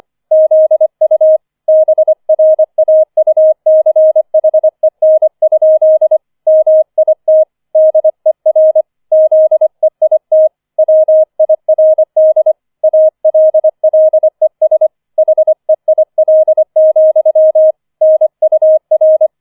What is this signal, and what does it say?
Morse code, heard as a keyed tone, about 25 wpm, 630 Hz. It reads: ZU BRAUCHEN? MIT DER ZEIT WIRD ALLES HEIL, NUR
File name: 004 Vierter Streich 25wpm